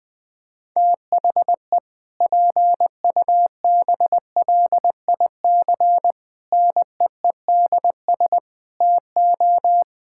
Morse code, heard as a keyed tone, about 21 wpm, 700 Hz. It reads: THE PUBLIC NEEDS TO
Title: Sentences with 2-Word Phrases - Spoken Beginning - 20wpm